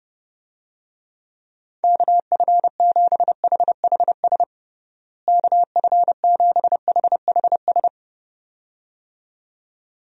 Morse code, heard as a keyed tone, about 31 wpm, 700 Hz. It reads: KF7HHS KF7HHS
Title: USA Call Signs - 6 Characters - Repeated 2x - 30wpm